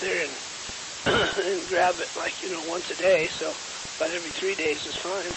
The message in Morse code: none

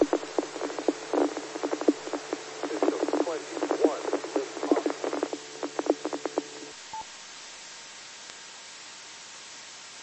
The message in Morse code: none

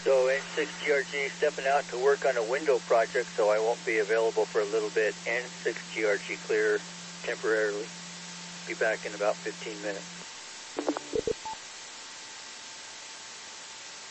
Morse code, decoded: none

